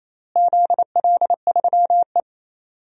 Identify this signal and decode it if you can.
Morse code, heard as a keyed tone, about 27 wpm, 700 Hz. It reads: ZL3E
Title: DXCC-100 Call Signs - 4 Characters - Repeated 3x - 28wpm